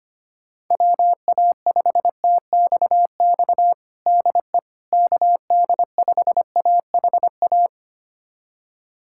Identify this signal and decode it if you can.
Morse code, heard as a keyed tone, about 25 wpm, 700 Hz. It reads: WA5TXX DE KD5AHA
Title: USA Callsign DE Callsign - 6 Characters - 25wpm